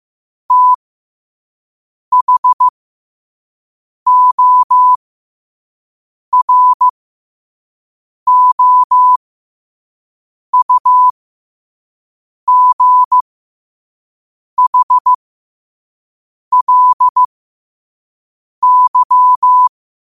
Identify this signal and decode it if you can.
Morse code, heard as a keyed tone, about 15 wpm, spaced as for 5 wpm, 1000 Hz. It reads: THOROUGHLY
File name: qotdfiles / 5wpm